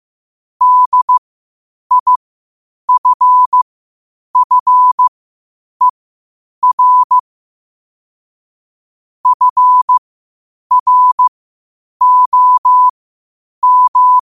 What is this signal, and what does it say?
Morse code, heard as a keyed tone, about 15 wpm, spaced as for 8 wpm, 1000 Hz. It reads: DIFFER FROM
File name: qotdfiles / 8wpm